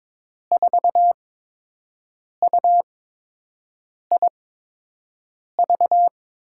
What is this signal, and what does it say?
Morse code, heard as a keyed tone, about 21 wpm, spaced as for 6 wpm, 700 Hz. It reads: 4UIV
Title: USA Call Signs - 5 Characters - 8x Character Spacing - 22wpm